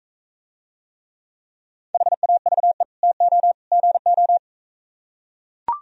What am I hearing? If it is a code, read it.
Morse code, heard as a keyed tone, about 42 wpm, 700 Hz. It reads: HAVE TO GO